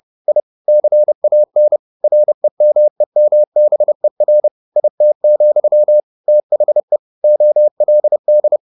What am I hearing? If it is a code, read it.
Morse code, heard as a keyed tone, about 30 wpm, 600 Hz. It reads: I CAN REMEMBER IT, THE OLD
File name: the old man and the sea / the old man and the sea 1 30wpm